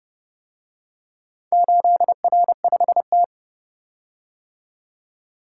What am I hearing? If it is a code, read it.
Morse code, heard as a keyed tone, about 30 wpm, 700 Hz. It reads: 8R5T